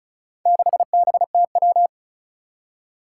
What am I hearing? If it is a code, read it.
Morse code, heard as a keyed tone, about 36 wpm, 700 Hz. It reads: 6BTW
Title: Four Challenging Letter-Number Combinations 35wpm